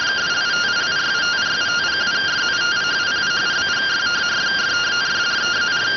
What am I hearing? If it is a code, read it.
Two-tone teleprinter signal, about 170 Hz shift.